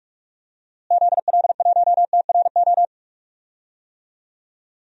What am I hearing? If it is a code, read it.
Morse code, heard as a keyed tone, about 46 wpm, 700 Hz. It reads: ZL1TRO
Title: DXCC-100 Call Signs - 6 Characters - Repeated 2x - 45wpm